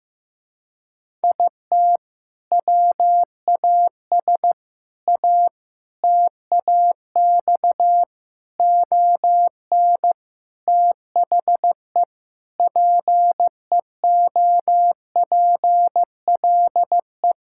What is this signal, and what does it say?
Morse code, heard as a keyed tone, about 15 wpm, 700 Hz. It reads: IT WAS A TAX ON THE PEOPLE
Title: Sentences from Top 1000 Words - Review Part 5 - 15wpm